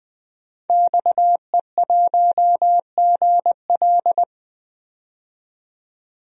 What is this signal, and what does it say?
Morse code, heard as a keyed tone, about 20 wpm, 700 Hz. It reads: XE1GL
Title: DXCC-100 Call Signs - 5 Characters - 20wpm